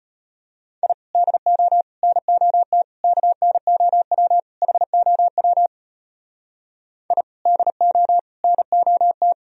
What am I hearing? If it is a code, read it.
Morse code, heard as a keyed tone, about 33 wpm, 700 Hz. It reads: I DO NOT KNOW HOW I DO NOT